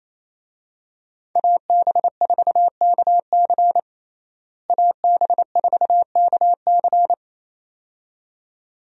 Morse code, heard as a keyed tone, about 28 wpm, 700 Hz. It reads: AB4KC AB4KC
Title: USA Call Signs - 5 Characters - Repeated 2x - 28wpm